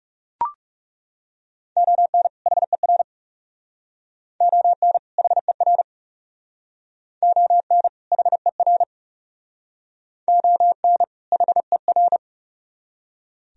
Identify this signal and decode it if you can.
Morse code, heard as a keyed tone, about 35 wpm, 700 Hz. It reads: ON HER ON HER ON HER ON HER
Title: Speed-Racing - Sets of 2 Words 30wpm